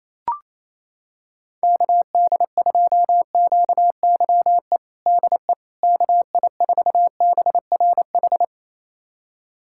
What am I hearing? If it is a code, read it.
Morse code, heard as a keyed tone, about 28 wpm, 700 Hz. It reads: KD2QYE DE KI4BRH
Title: USA Callsign DE Callsign 28wpm